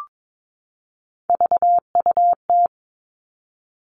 Morse code, heard as a keyed tone, about 22 wpm, 700 Hz. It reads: VUT
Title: All Cut Numbers - 3-Digit 22wpm